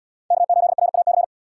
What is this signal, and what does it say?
Morse code, heard as a keyed tone, about 73 wpm, 700 Hz. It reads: K8RAC